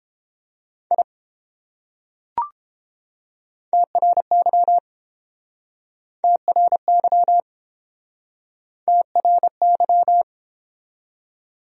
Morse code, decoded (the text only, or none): I TRY TRY TRY